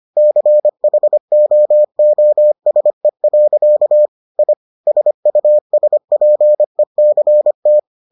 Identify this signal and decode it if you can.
Morse code, heard as a keyed tone, about 25 wpm, 600 Hz. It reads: CHOOSE. I SUSPECT